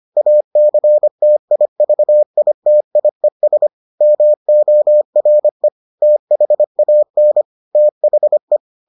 Morse code, heard as a keyed tone, about 25 wpm, 600 Hz. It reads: ACTIVITIES MORE THAN THE